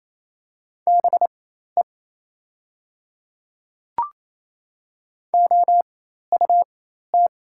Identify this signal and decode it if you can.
Morse code, heard as a keyed tone, about 28 wpm, spaced as for 13 wpm, 700 Hz. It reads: BE OUT
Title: 4x Character Spacing - Top 100 Words - 28wpm